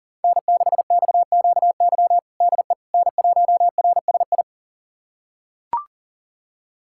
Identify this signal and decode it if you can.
Morse code, heard as a keyed tone, about 40 wpm, 700 Hz. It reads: N6XQY DE N1RSI
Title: USA Callsign DE Callsign - 5 Characters - 40wpm